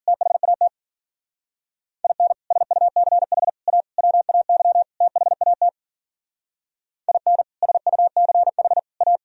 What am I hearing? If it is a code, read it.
Morse code, heard as a keyed tone, about 50 wpm, 700 Hz. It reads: THAT IN SUCH A WAY THAT IN SUCH A